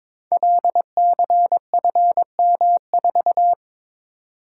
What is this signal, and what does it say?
Morse code, heard as a keyed tone, about 22 wpm, 700 Hz. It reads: LCFM4
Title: Code Groups - 5 Letter:Numbers - 22wpm